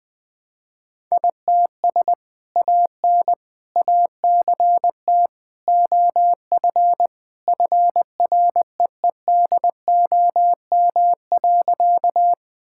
Morse code, heard as a keyed tone, about 20 wpm, 700 Hz. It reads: ITS AN ACT OF FREEDOM.
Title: Sentences from Top 1000 Words - Review Part 5 - 20wpm